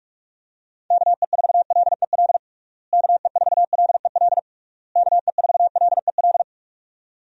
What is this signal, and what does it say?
Morse code, heard as a keyed tone, about 41 wpm, 700 Hz. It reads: KE4LEL KE4LEL KE4LEL
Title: USA Call Signs - 6 Characters - Repeated 3x - 45wpm